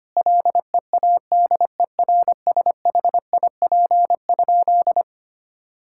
Morse code, heard as a keyed tone, about 25 wpm, 700 Hz. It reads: LEADERSHIP?
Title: Sentences from Top 1600 Words 25wpm